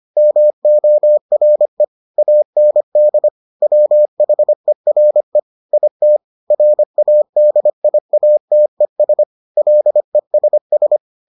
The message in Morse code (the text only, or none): MORE AND WHERE IT RADIATES LESS